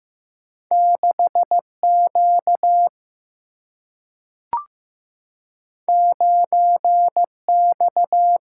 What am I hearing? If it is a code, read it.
Morse code, heard as a keyed tone, about 15 wpm, 700 Hz. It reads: 6Q 9X
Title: Single-Letter Single-Number 15wpm